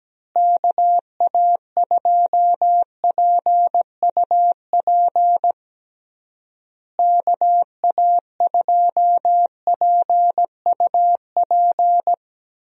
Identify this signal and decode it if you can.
Morse code, heard as a keyed tone, about 17 wpm, 700 Hz. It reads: KA2PUP KA2PUP
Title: USA Call Signs - 6 Characters - Repeated 3x - 17wpm